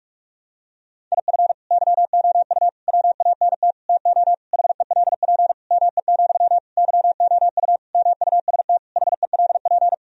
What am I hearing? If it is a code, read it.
Morse code, heard as a keyed tone, about 41 wpm, 700 Hz. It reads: IF YOU WANT TO HELP ME, YOU MUST HELP
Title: Sentences from Top 1000 Words - Review Part 1 - 45wpm